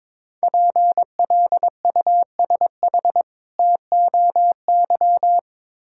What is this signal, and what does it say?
Morse code, heard as a keyed tone, about 22 wpm, 700 Hz. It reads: PLUSH TOY
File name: Difficult Words with Hints - 22wpm